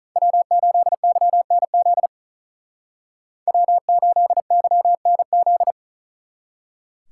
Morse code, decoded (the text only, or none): W8YNZ W8YNZ